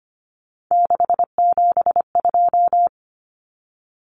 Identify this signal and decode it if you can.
Morse code, heard as a keyed tone, about 25 wpm, 700 Hz. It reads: 672
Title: Numbers - 3-Digit 25wpm